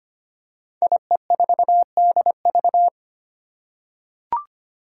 Morse code, heard as a keyed tone, about 25 wpm, 700 Hz. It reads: IE4DV
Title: Five Challenging Letter-Number Combinations 25wpm